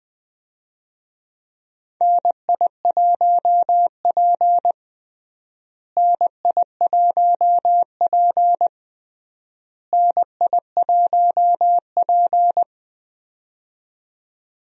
Morse code, heard as a keyed tone, about 19 wpm, 700 Hz. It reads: NI1P NI1P NI1P
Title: USA Call Signs - 4 Characters - Repeated 3x - 20wpm